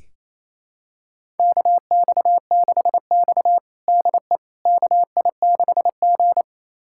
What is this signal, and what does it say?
Morse code, heard as a keyed tone, about 28 wpm, 700 Hz. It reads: KX6X DE KI6G
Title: USA Callsign DE Callsign - 4 Characters - 28wpm